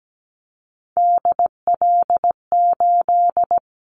Morse code, heard as a keyed tone, about 17 wpm, 700 Hz. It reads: DL8
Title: Callsign Prefixes for Top 50 DXCC - 17wpm